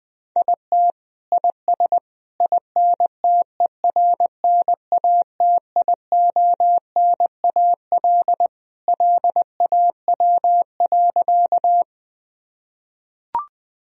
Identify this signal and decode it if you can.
Morse code, heard as a keyed tone, about 20 wpm, 700 Hz. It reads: IT IS INTERNATIONAL LAW.
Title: Sentences from Top 1000 Words - Review Part 4 - 20wpm